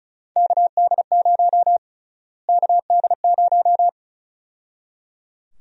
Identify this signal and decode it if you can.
Morse code, heard as a keyed tone, about 34 wpm, 700 Hz. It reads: KD0 KD0
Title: USA Call Sign Prefixes - Repeated 2x - 35wpm